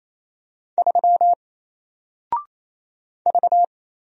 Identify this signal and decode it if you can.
Morse code, heard as a keyed tone, about 28 wpm, 700 Hz. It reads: 3 V
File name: Single-Letter-Number-28wpm